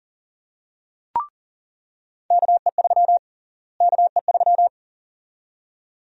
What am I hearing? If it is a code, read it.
Morse code, heard as a keyed tone, about 40 wpm, 700 Hz. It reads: KE3 KE3
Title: USA Call Sign Prefixes - Repeated 2x - 40wpm